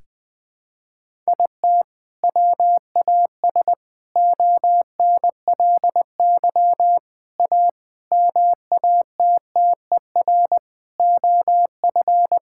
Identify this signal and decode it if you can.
Morse code, heard as a keyed tone, about 20 wpm, 700 Hz. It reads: IT WAS ONLY A MATTER OF
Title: Sentences with 8-Word Phrases - 20wpm